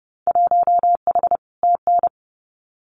Morse code, heard as a keyed tone, about 30 wpm, 700 Hz. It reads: 1H TN
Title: Winter Field Day Exchanges 30wpm